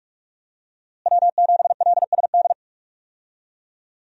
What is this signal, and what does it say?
Morse code, heard as a keyed tone, about 41 wpm, 700 Hz. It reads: W7LSD
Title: USA Call Signs - 5 Characters - Repeated 3x - 45wpm